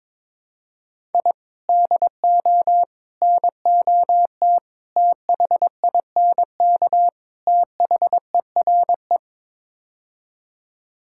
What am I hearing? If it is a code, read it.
Morse code, heard as a keyed tone, about 22 wpm, 700 Hz. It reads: I DO NOT THINK THERE
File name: practice / 22wpm / Build A Phrase - 5-Word Phrases - 22wpm